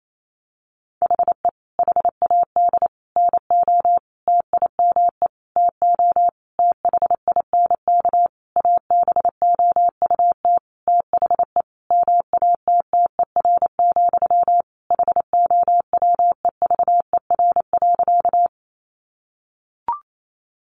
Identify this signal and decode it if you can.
Morse code, heard as a keyed tone, about 28 wpm, 700 Hz. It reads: HE HAD NO TIME TO THINK ABOUT THE MATTER, HOWEVER.
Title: Sentences from Top 700 Words 28wpm